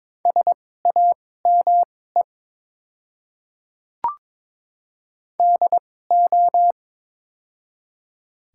Morse code, heard as a keyed tone, about 22 wpm, spaced as for 16 wpm, 690 Hz. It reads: SAME DO
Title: 2x Character Spacing - Top 200 Words - 22wpm